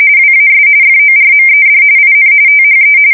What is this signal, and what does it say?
RTTY, about 45.45 baud. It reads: WAS FORMED IN THE
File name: fsk170 / usnavy2